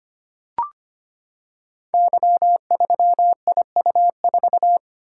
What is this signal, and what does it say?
Morse code, heard as a keyed tone, about 25 wpm, 700 Hz. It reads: Y3IU4
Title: Code Groups - 5 Letter:Numbers - 25wpm